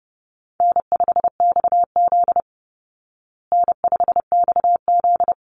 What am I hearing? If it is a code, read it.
Morse code, heard as a keyed tone, about 28 wpm, 700 Hz. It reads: N5XZ N5XZ
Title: Easy Call Signs Repeated 2x - 30wpm